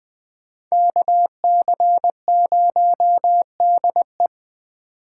Morse code, heard as a keyed tone, about 21 wpm, 700 Hz. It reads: KC0DE